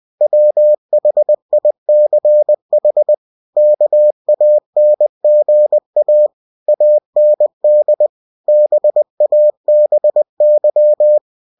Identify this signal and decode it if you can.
Morse code, heard as a keyed tone, about 20 wpm, 600 Hz. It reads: WHICH KANGA AND BABY